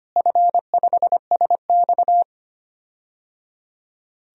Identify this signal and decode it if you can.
Morse code, heard as a keyed tone, about 25 wpm, 700 Hz. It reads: F5SX
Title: DXCC-100 Call Signs - 4 Characters - Repeated 3x - 25wpm